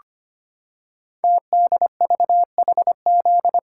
Morse code, heard as a keyed tone, about 25 wpm, 700 Hz. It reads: TDVHZ
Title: Code Groups - 5 Letters - 25wpm